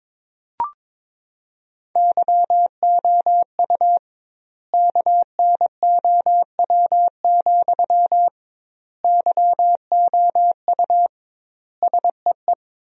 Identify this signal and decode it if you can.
Morse code, heard as a keyed tone, about 22 wpm, 700 Hz. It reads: YOU KNOW, YOU SEE
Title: 2x Word-Spacing - Sentences from Top 200 Words - 22wpm